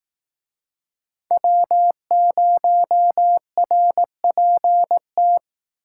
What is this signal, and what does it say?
Morse code, heard as a keyed tone, about 19 wpm, 700 Hz. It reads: W0RPT